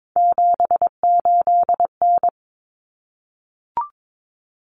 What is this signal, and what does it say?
Morse code, heard as a keyed tone, about 22 wpm, 700 Hz. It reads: 78N
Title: Common Cut Numbers - 3-Digit 22wpm